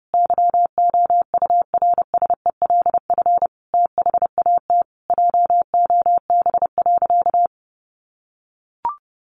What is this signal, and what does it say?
Morse code, heard as a keyed tone, about 28 wpm, 700 Hz. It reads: YOURSELF THAT JOB.
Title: Sentences from Top 1100 Words 30wpm